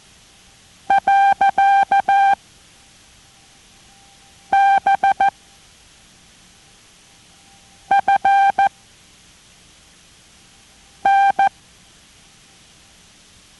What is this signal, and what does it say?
Morse code, heard as a keyed tone, about 14 wpm, 790 Hz. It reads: . B F N